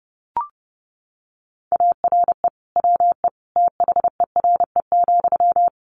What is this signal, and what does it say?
Morse code, heard as a keyed tone, about 30 wpm, 700 Hz. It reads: ARE WE THERE,
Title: Sentences from Top 1300 Words 30wpm